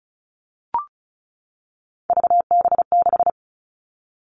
Morse code, heard as a keyed tone, about 36 wpm, 700 Hz. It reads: VB6
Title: Three Challenging Letter-Number Combinations 35wpm